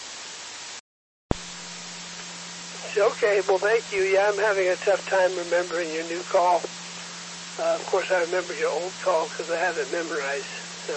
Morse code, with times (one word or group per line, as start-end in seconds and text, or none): none